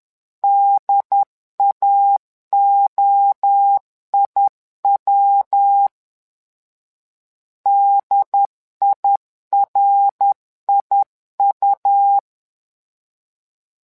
Morse code, heard as a keyed tone, about 10 wpm, 800 Hz. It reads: DAOIW DIRIU
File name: LEZIONI / 3 MWRO / 3 MWRO 10wpm 2148